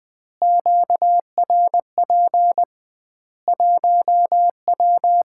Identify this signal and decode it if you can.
Morse code, heard as a keyed tone, about 21 wpm, 700 Hz. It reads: QRP 1W
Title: Common QSO Phrases - 2x Word-Spacing - 20wpm